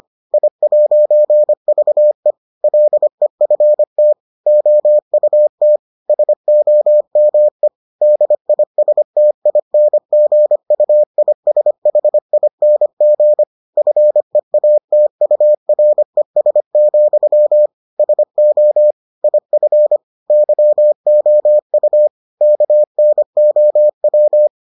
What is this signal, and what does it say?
Morse code, heard as a keyed tone, about 25 wpm, 600 Hz. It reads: I'VE LEFT OUT SOME DISTINGUISHING FEATURES, SO IF YOU KNOW